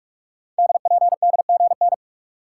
Morse code, heard as a keyed tone, about 44 wpm, 700 Hz. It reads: DPDGN
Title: Code Groups - 5 Letters - 45wpm